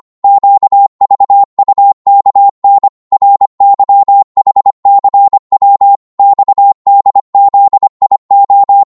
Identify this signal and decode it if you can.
Morse code, heard as a keyed tone, about 25 wpm, 800 Hz. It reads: QVUKN RYHCW XDZIO
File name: Morse / ltr-25wpm